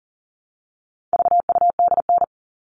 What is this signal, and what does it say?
Morse code, heard as a keyed tone, about 40 wpm, 700 Hz. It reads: VUDN
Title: All Cut Numbers - 4-Digit 40wpm